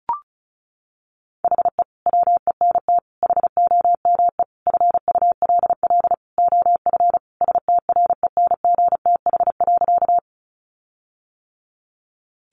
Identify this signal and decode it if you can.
Morse code, heard as a keyed tone, about 34 wpm, 700 Hz. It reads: HE WENT HOME FULL OF STRENGTH.
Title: Sentences from Top 1300 Words 35wpm